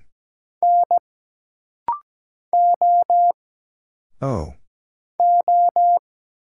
Morse code, heard as a keyed tone, about 17 wpm, 700 Hz. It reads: N O O
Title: Single Letter-Number Alt - Rapid-Fire 17wpm